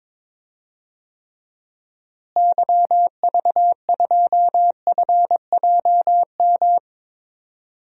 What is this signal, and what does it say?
Morse code, heard as a keyed tone, about 21 wpm, 700 Hz. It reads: YV2FJM
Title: DXCC-100 Call Signs - 6 Characters - Repeated 2x - 22wpm